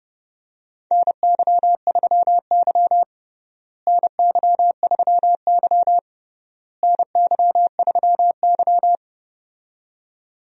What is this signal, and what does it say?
Morse code, heard as a keyed tone, about 30 wpm, 700 Hz. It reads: NY3Y NY3Y NY3Y
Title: USA Call Signs - 4 Characters - Repeated 3x - 30wpm